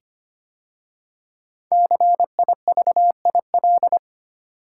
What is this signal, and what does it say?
Morse code, heard as a keyed tone, about 25 wpm, 700 Hz. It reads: CIVIL
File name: 5 Letter Words - First 3 Letters Given - 25wpm